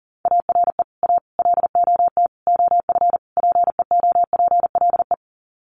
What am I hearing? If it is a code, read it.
Morse code, heard as a keyed tone, about 40 wpm, 700 Hz. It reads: ARE A LOT OF PEOPLE
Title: Sets of 5 Words 40wpm